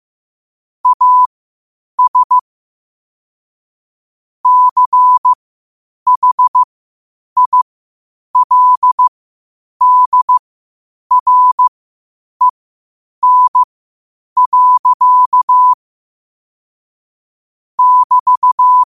Morse code, heard as a keyed tone, about 15 wpm, spaced as for 8 wpm, 1000 Hz. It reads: AS CHILDREN. <BT>